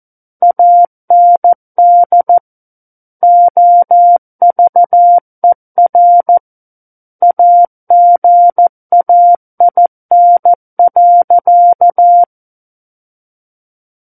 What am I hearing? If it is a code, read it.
Morse code, heard as a keyed tone, about 14 wpm, 700 Hz. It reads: AND OVER AGAIN.
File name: Book One / 05-Chapter 5 - 14 WPM